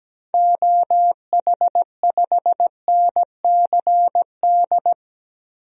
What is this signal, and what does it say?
Morse code, heard as a keyed tone, about 17 wpm, 700 Hz. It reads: OH5NCD
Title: DXCC-100 Call Signs - 6 Characters - Repeated 3x - 17wpm